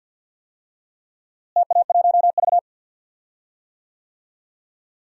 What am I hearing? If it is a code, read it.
Morse code, heard as a keyed tone, about 50 wpm, 700 Hz. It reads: TA1V
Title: DXCC-100 Call Signs - 4 Characters - Repeated 3x - 50wpm